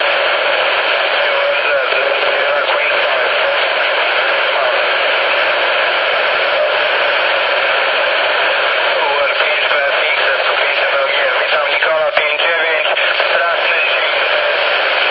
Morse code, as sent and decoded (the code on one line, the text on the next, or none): none